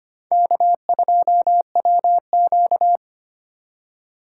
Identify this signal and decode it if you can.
Morse code, heard as a keyed tone, about 25 wpm, 700 Hz. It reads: K2WQ
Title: USA Call Signs - 4 Characters - 4x Character Spacing - 25wpm